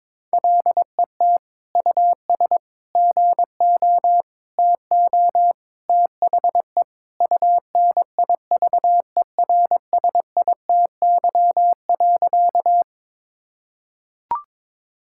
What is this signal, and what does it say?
Morse code, heard as a keyed tone, about 21 wpm, 700 Hz. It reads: LET US GO TO THE UNIVERSITY.
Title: Sentences from Top 1000 Words - Review Part 5 - 22wpm